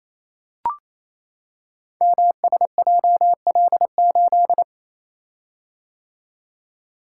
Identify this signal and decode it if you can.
Morse code, heard as a keyed tone, about 28 wpm, 700 Hz. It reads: MSJL8
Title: Code Groups - 5 Letter:Numbers - 28wpm